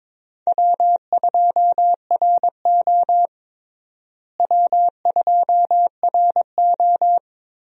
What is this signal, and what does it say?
Morse code, heard as a keyed tone, about 22 wpm, 700 Hz. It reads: W2RO W2RO
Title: USA Call Signs - 4 Characters - Repeated 3x - 22wpm